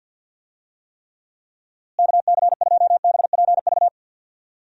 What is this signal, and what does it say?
Morse code, heard as a keyed tone, about 51 wpm, 700 Hz. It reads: KC2BPV
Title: USA Call Signs - 6 Characters - Repeated 2x - 50wpm